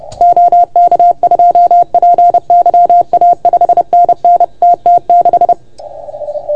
Morse code, heard as a keyed tone, about 30 wpm, 690 Hz. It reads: OK2PYA 5NN TT6
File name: Archiv / CUC381 / OK1FGD CUC 381